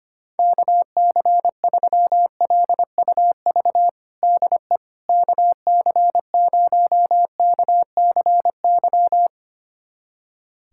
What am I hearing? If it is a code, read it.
Morse code, heard as a keyed tone, about 25 wpm, 700 Hz. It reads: KC3LUV DE KC0KCY
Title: USA Callsign DE Callsign 25wpm